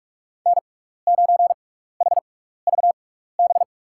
Morse code, heard as a keyed tone, about 47 wpm, spaced as for 16 wpm, 700 Hz. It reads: N9HVB
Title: USA Call Signs - 5 Characters - 6x Character Spacing - 45wpm